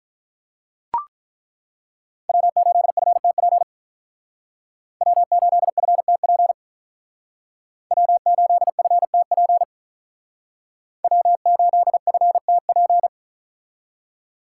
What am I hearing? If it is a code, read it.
Morse code, heard as a keyed tone, about 42 wpm, 700 Hz. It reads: W8FTP W8FTP W8FTP W8FTP
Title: Speed Racing - USA 1x3 Call Signs 35wpm